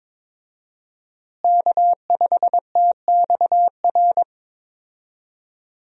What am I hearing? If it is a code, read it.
Morse code, heard as a keyed tone, about 22 wpm, 700 Hz. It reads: K5TXR
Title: USA Call Signs - 5 Characters - Repeated 2x - 22wpm